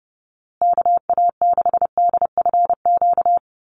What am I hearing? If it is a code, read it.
Morse code, heard as a keyed tone, about 30 wpm, 700 Hz. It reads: KA6DFQ
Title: USA Call Signs Repeated 2x 30wpm